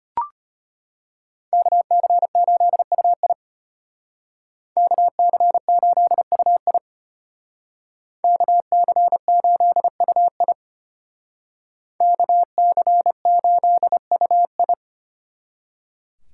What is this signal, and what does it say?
Morse code, heard as a keyed tone, about 33 wpm, 700 Hz. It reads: KC8UI KC8UI KC8UI KC8UI